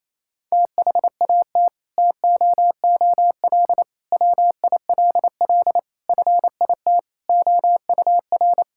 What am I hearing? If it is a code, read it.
Morse code, heard as a keyed tone, about 28 wpm, 700 Hz. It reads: THAT TOOL WILL FIT OUR
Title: Sentences from Top 1000 Words - Review Part 6 - 28wpm